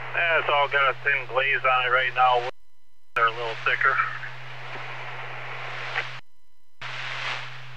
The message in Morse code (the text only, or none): none